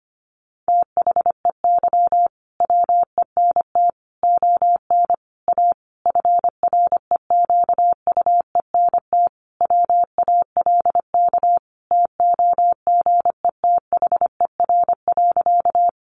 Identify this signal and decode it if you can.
Morse code, heard as a keyed tone, about 25 wpm, 700 Hz. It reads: THEY WENT ON A FREQUENT WALK TOGETHER.
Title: Sentences from Top 1400 Words 25wpm